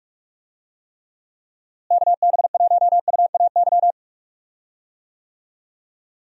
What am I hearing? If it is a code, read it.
Morse code, heard as a keyed tone, about 42 wpm, 700 Hz. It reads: KB1UAY
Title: USA Call Signs - 6 Characters - Repeated 2x - 45wpm